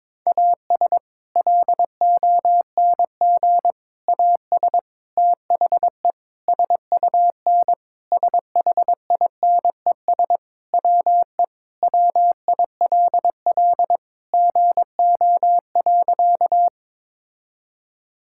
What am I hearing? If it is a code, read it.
Morse code, heard as a keyed tone, about 22 wpm, 700 Hz. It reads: AS LONG AS THE SUN SHINES WE WILL GO.
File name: practice / 22wpm / Sentences with 4-Word Phrases - 22wpm